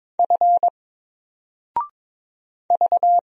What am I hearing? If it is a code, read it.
Morse code, heard as a keyed tone, about 22 wpm, 700 Hz. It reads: F V
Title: Single Character (Letter:Number:Punctuation:Prosign) 22wpm